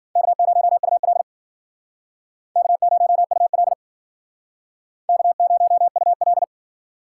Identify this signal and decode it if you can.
Morse code, heard as a keyed tone, about 53 wpm, 700 Hz. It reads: K0UL K0UL K0UL